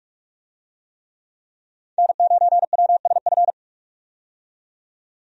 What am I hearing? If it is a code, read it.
Morse code, heard as a keyed tone, about 46 wpm, 700 Hz. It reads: N9WSF
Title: USA Call Signs - 5 Characters - 45wpm